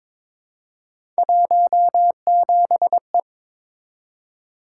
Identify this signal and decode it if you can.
Morse code, heard as a keyed tone, about 22 wpm, 700 Hz. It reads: 17E
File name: practice / 22wpm / Three Letter-Number Combinations - 22wpm